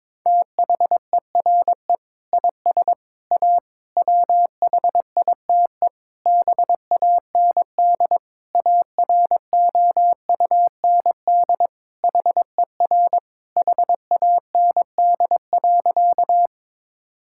Morse code, heard as a keyed tone, about 22 wpm, 700 Hz. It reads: THERE IS A WHITE BAND AROUND HER HAND.
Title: Sentences from Top 1600 Words 22wpm